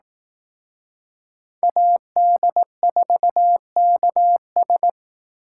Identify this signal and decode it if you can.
Morse code, heard as a keyed tone, about 18 wpm, 700 Hz. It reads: AD4KS
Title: Speed Racing - USA 2x2 Call Signs 15wpm